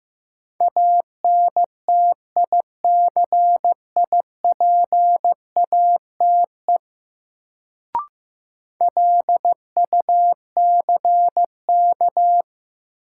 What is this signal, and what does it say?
Morse code, heard as a keyed tone, about 15 wpm, 700 Hz. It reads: ANTICIPATE LUCK
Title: Sentences from Top 1500 Words - New Word Drill Alt - 15wpm